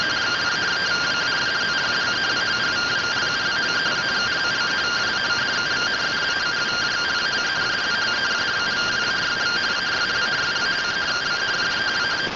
Two-tone teleprinter signal, about 170 Hz shift.